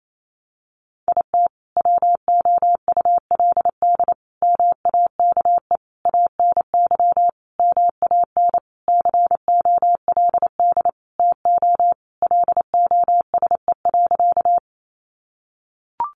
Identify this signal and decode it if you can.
Morse code, heard as a keyed tone, about 28 wpm, 700 Hz. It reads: IT WOULD MAKE ANY MAN COLD TO LOSE.
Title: Sentences from Top 1100 Words 28wpm